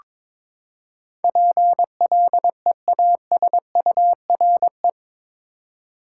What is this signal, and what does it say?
Morse code, heard as a keyed tone, about 21 wpm, 700 Hz. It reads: PLEASURE
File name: practice / 22wpm / Eight Letter Words - 22wpm